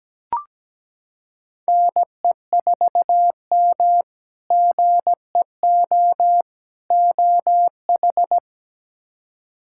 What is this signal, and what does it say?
Morse code, heard as a keyed tone, about 17 wpm, 700 Hz. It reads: NE4M GEO OH
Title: K1USN SST QSO Practice - 17wpm